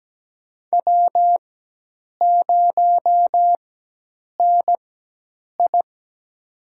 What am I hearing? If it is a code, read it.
Morse code, heard as a keyed tone, about 17 wpm, spaced as for 8 wpm, 700 Hz. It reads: W0NI
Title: USA Call Signs - 4 Characters - 4x Character Spacing - 17wpm